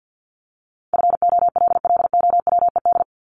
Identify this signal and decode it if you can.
Morse code, heard as a keyed tone, about 50 wpm, 700 Hz. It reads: FOLLOWED
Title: ED Suffix - Top 100 Words with Base Words 50wpm